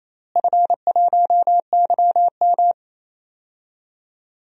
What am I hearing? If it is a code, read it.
Morse code, heard as a keyed tone, about 28 wpm, 700 Hz. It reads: F1YM
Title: DXCC-100 Call Signs - 4 Characters - Repeated 2x - 28wpm